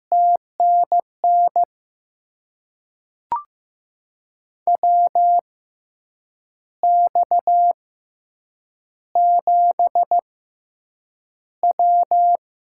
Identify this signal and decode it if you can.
Morse code, heard as a keyed tone, about 15 wpm, spaced as for 7 wpm, 700 Hz. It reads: TNN WX7W
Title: USA Call Signs - 6x Character Spacing - 15wpm